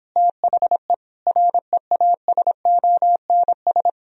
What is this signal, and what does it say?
Morse code, heard as a keyed tone, about 29 wpm, 700 Hz. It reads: THE REASONS